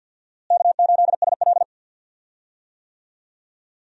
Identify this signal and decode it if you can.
Morse code, heard as a keyed tone, about 53 wpm, 700 Hz. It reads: K8SL
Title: USA Call Signs - 4 Characters - Repeated 2x - 50wpm